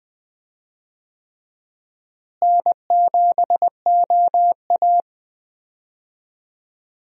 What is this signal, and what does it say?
Morse code, heard as a keyed tone, about 21 wpm, 700 Hz. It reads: N7OA
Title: USA Call Signs - 4 Characters - Repeated 3x - 20wpm